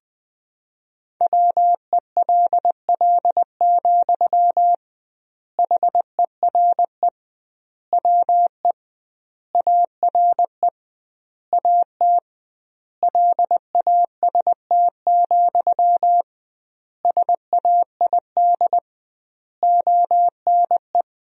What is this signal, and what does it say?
Morse code, heard as a keyed tone, about 21 wpm, 700 Hz. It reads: WELL, HERE WE ARE AT LAST, SAID ONE
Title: 2x Word-Spacing - Sentences from Top 300 Words - 20wpm